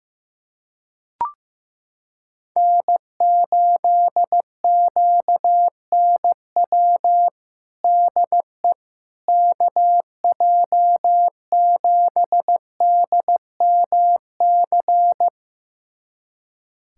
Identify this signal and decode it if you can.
Morse code, heard as a keyed tone, about 15 wpm, 700 Hz. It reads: N8QNW DE KJ7DMC
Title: USA Callsign DE Callsign 15wpm